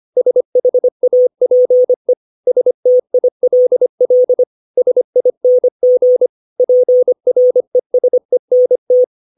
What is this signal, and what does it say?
Morse code, heard as a keyed tone, about 25 wpm, 500 Hz. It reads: SHAPE STILL SING PRESENT
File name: 632 words / 632 Words 25WPM